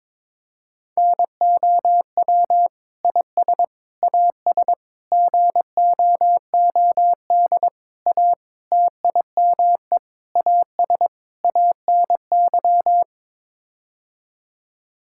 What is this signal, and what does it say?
Morse code, heard as a keyed tone, about 22 wpm, 700 Hz. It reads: NOW IS AS GOOD A TIME AS ANY
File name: practice / 22wpm / Sets of 8 Words - 22wpm